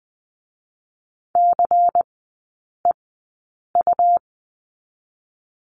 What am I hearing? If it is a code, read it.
Morse code, heard as a keyed tone, about 20 wpm, 700 Hz. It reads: C E U
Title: 212 Lesson 34e - Number 8 - Three Characters 20wpm